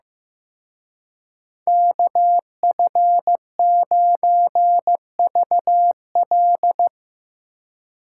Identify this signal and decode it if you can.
Morse code, heard as a keyed tone, about 17 wpm, 700 Hz. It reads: KF9VL
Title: Speed Racing - USA 2x2 Call Signs 15wpm